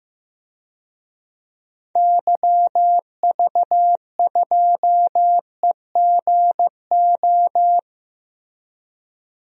Morse code, heard as a keyed tone, about 15 wpm, 700 Hz. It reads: YV2EGO
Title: DXCC-100 Call Signs - 6 Characters - 15wpm